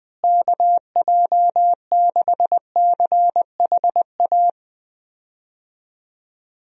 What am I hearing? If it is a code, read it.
Morse code, heard as a keyed tone, about 20 wpm, 700 Hz. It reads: KJ6CHA
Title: USA Call Signs - 6 Characters - Repeated 2x - 20wpm